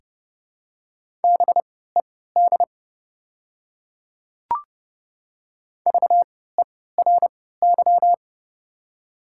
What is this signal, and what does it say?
Morse code, heard as a keyed tone, about 30 wpm, spaced as for 17 wpm, 700 Hz. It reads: BED VERY
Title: 3x Character Spacing - Top 500 Words - 30wpm